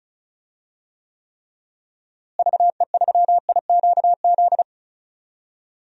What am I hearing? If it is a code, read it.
Morse code, heard as a keyed tone, about 36 wpm, 700 Hz. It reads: VE3IQZ
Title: DXCC-100 Call Signs - 6 Characters - Repeated 3x - 35wpm